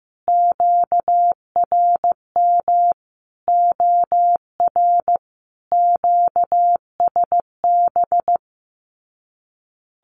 Morse code, heard as a keyed tone, about 15 wpm, 700 Hz. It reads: QRM OR QSB
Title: Common QSO Phrases 15wpm